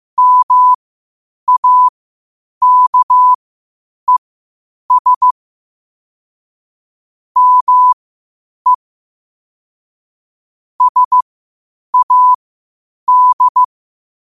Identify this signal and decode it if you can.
Morse code, heard as a keyed tone, about 15 wpm, spaced as for 8 wpm, 1000 Hz. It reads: MAKES ME SAD
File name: qotdfiles / 8wpm